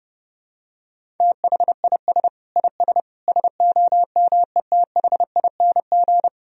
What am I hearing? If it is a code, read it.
Morse code, heard as a keyed tone, about 32 wpm, 700 Hz. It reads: THIS IS SOMETHING
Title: Sentences from Top 1000 Words - Review Part 1 - 30wpm